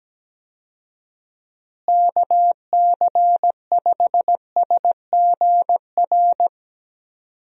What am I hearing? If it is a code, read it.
Morse code, heard as a keyed tone, about 17 wpm, 700 Hz. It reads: KC5SGR
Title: USA Call Signs - 6 Characters - Repeated 3x - 17wpm